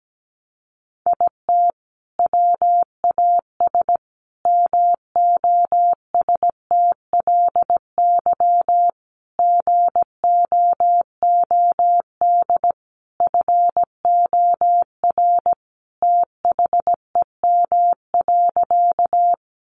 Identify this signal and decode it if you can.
Morse code, heard as a keyed tone, about 17 wpm, 700 Hz. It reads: IT WAS MOSTLY GOOD FOR THEM.
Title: Sentences from Top 1300 Words 17wpm